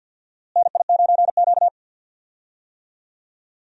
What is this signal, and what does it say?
Morse code, heard as a keyed tone, about 53 wpm, 700 Hz. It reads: NI9Q
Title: USA Call Signs - 4 Characters - Repeated 2x - 50wpm